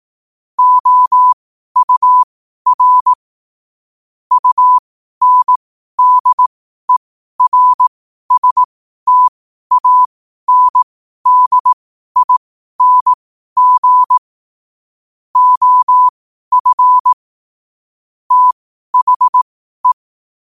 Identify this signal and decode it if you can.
Morse code, heard as a keyed tone, about 18 wpm, spaced as for 12 wpm, 1000 Hz. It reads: OUR UNDERSTANDING OF THE
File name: qotdfiles / 12wpm